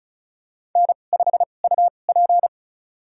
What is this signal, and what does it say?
Morse code, heard as a keyed tone, about 36 wpm, spaced as for 25 wpm, 700 Hz. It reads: N5UP
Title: USA Call Signs - 4 Characters - 2x Character Spacing - 35wpm